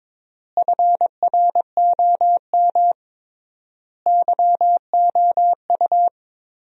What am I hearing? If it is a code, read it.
Morse code, heard as a keyed tone, about 22 wpm, 700 Hz. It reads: FROM YOU
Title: 3x Word-Spacing - Sentences from Top 400 Words - 22wpm